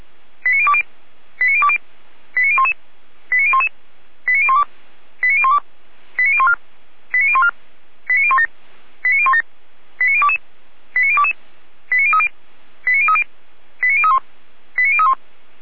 Five-tone selective call (ZVEI), heard as quick runs of five tones, tones of about 70 ms. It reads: 89020 89020 89011 89011 89012 89012 89014 89014 89017 89017 89022 89022 89030 89030 89031 89031